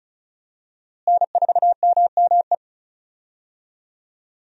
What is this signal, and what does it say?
Morse code, heard as a keyed tone, about 36 wpm, 700 Hz. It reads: N4MME
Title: USA Call Signs - 5 Characters - Repeated 3x - 35wpm